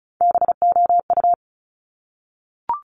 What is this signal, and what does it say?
Morse code, heard as a keyed tone, about 36 wpm, 700 Hz. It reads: BOU
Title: Top Three Letter Combinations in Words 35wpm